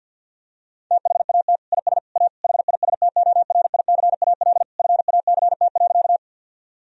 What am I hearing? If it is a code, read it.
Morse code, heard as a keyed tone, about 50 wpm, 700 Hz. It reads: THAT IS A HISTORICAL FACT.
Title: Sentences from Top 1000 Words - Review Part 4 - 50wpm